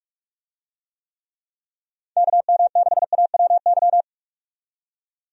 Morse code, heard as a keyed tone, about 46 wpm, 700 Hz. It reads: KM6AWY
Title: USA Call Signs - 6 Characters - 45wpm